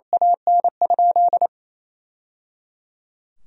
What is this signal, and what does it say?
Morse code, heard as a keyed tone, about 28 wpm, 700 Hz. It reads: AN?
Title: Sentences from Top 1000 Words - Review Part 3 - 28wpm